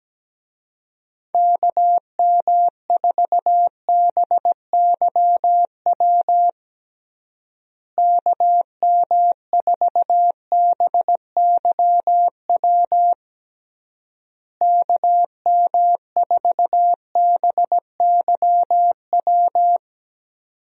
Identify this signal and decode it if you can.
Morse code, heard as a keyed tone, about 17 wpm, 700 Hz. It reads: KM4BYW KM4BYW KM4BYW
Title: USA Call Signs - 6 Characters - Repeated 3x - 17wpm